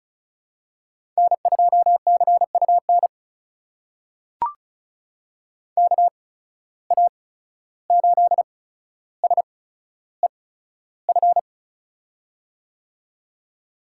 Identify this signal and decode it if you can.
Morse code, heard as a keyed tone, about 36 wpm, spaced as for 14 wpm, 700 Hz. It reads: N2CUN KA8SEF
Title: USA Call Signs - 8x Character Spacing - 35wpm